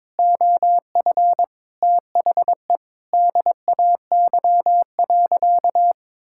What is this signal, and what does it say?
Morse code, heard as a keyed tone, about 22 wpm, 700 Hz. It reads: OF THE DAY.
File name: Sentences with 9-Word Phrases - 22wpm